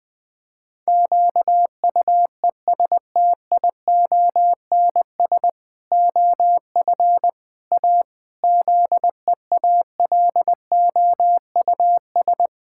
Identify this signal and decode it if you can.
Morse code, heard as a keyed tone, about 21 wpm, 700 Hz. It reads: QUESTIONS OF A ZEALOUS
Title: Pangram Challenge v2 - 20wpm